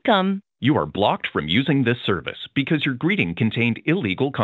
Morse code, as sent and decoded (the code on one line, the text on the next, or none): none